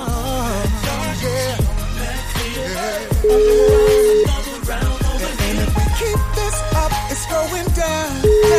Ringback tone (Europe, 425 Hz), with noise.